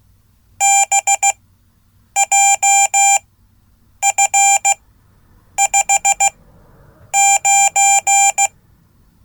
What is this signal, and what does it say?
Morse code, heard as a keyed tone, about 15 wpm, spaced as for 8 wpm, 790 Hz. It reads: BJF59
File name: Code / 5min-a